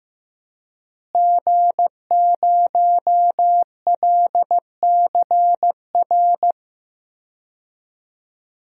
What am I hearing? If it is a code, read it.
Morse code, heard as a keyed tone, about 15 wpm, 700 Hz. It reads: G0LCR
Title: DXCC-100 Call Signs - 5 Characters - Repeated 2x - 15wpm